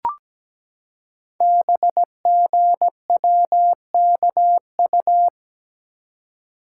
Morse code, heard as a keyed tone, about 17 wpm, 700 Hz. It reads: BGWKU
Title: Code Groups - 5 Letters - 17wpm